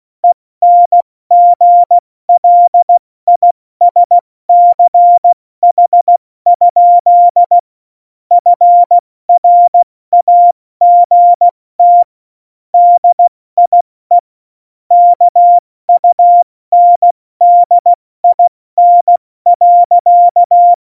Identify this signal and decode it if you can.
Morse code, heard as a keyed tone, about 16 wpm, 700 Hz. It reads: ENGLISCH? FRAGT DIE KUNDIN.
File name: Morsekurs / Witze / Witz 20. November 2018, 70 F80